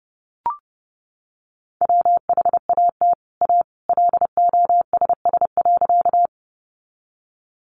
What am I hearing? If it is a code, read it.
Morse code, heard as a keyed tone, about 30 wpm, 700 Hz. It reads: WHAT A LOSS.
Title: Sentences from Top 1400 Words 30wpm